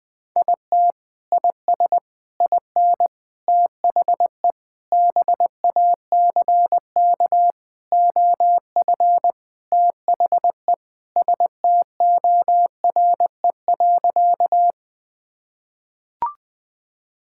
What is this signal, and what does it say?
Morse code, heard as a keyed tone, about 20 wpm, 700 Hz. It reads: IT IS IN THE BACK OF THE STORE.
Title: Sentences from Top 1000 Words - Review Part 5 - 20wpm